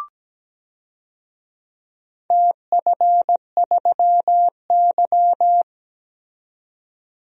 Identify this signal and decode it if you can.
Morse code, heard as a keyed tone, about 17 wpm, 700 Hz. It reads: TF3Y
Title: DXCC-100 Call Signs - 4 Characters - 17wpm